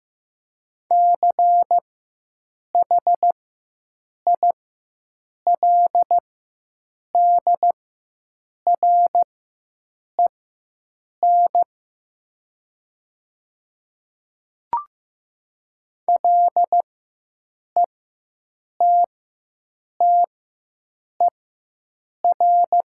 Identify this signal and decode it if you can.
Morse code, heard as a keyed tone, about 15 wpm, spaced as for 7 wpm, 700 Hz. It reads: CHILDREN LETTER
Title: 4x Character Spacing - Top 300 Words - 15wpm